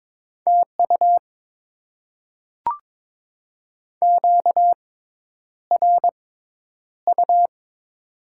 Morse code, heard as a keyed tone, about 22 wpm, spaced as for 10 wpm, 700 Hz. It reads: TU QRU
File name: Top 90 QSO Elements - 6x Character Spacing - 22wpm